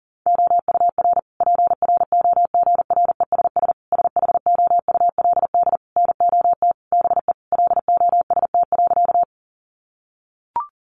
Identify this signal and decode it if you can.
Morse code, heard as a keyed tone, about 40 wpm, 700 Hz. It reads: OUR PROGRESS SHOULD NOT BE LOST.
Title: Sentences from Top 1400 Words 40wpm